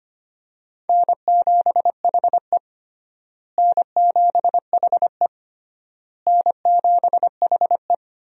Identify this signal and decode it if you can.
Morse code, heard as a keyed tone, about 25 wpm, 700 Hz. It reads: N7HE N7HE N7HE
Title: USA Call Signs - 4 Characters - Repeated 3x - 25wpm